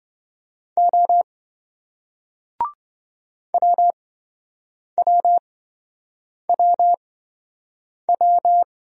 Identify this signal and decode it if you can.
Morse code, heard as a keyed tone, about 26 wpm, spaced as for 7 wpm, 700 Hz. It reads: O WWWW